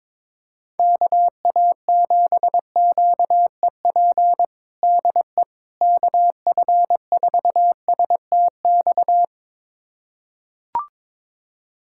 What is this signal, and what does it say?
Morse code, heard as a keyed tone, about 22 wpm, 700 Hz. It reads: KA7QEP DE KF4STX
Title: USA Callsign DE Callsign 22wpm